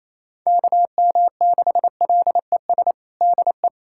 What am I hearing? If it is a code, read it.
Morse code, heard as a keyed tone, about 28 wpm, 700 Hz. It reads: KM6LES DE
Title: USA Callsign DE Callsign - 6 Characters - 28wpm